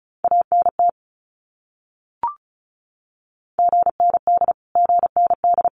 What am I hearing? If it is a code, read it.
Morse code, heard as a keyed tone, about 34 wpm, 700 Hz. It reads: ANT GND GND
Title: Abbreviations Repeated 3x - 35wpm